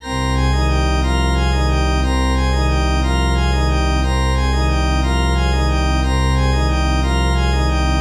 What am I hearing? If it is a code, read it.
There is no code.